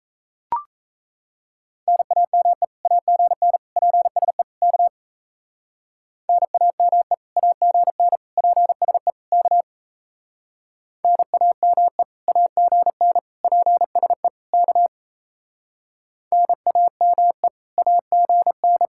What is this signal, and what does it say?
Morse code, heard as a keyed tone, about 38 wpm, 700 Hz. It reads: NAME AGN PSE K NAME AGN PSE K NAME AGN PSE K NAME AGN